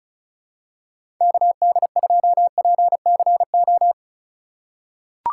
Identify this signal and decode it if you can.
Morse code, heard as a keyed tone, about 36 wpm, 700 Hz. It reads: KD2PCO
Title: USA Call Signs - 4x Character Spacing - 35wpm